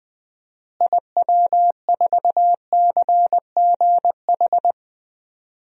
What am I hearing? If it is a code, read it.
Morse code, heard as a keyed tone, about 20 wpm, 700 Hz. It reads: IW4CGH
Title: DXCC-100 Call Signs - 6 Characters - Repeated 3x - 20wpm